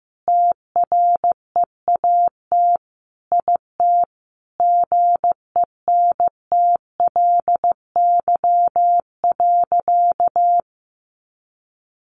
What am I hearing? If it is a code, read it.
Morse code, heard as a keyed tone, about 15 wpm, 700 Hz. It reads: TREAT IT GENTLY.
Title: Sentences from Top 1500 Words 15wpm